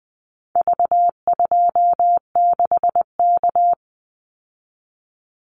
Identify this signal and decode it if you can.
Morse code, heard as a keyed tone, about 20 wpm, 700 Hz. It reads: V26K
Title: Call Signs Repeated 2x - 20wpm